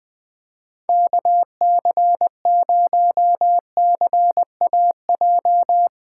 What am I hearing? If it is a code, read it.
Morse code, heard as a keyed tone, about 20 wpm, 700 Hz. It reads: KC0CAJ
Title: Speed Racing - USA 2x3 Call Signs 20wpm